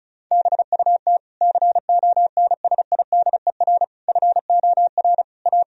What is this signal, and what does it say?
Morse code, heard as a keyed tone, about 36 wpm, 700 Hz. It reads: BUT CONSIDER FOR A
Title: Sentences from Top 1000 Words - Review Part 4 - 35wpm